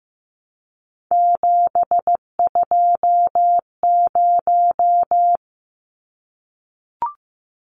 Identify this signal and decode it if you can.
Morse code, heard as a keyed tone, about 15 wpm, 700 Hz. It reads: 720
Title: Numbers - 3-Digit 15wpm